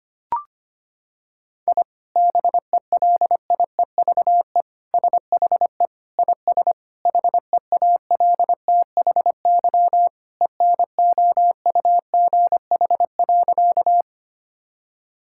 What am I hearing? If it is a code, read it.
Morse code, heard as a keyed tone, about 25 wpm, 700 Hz. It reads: I BELIEVE SHE IS HEALTHY ENOUGH.
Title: Sentences from Top 1000 Words - Review Part 5 - 25wpm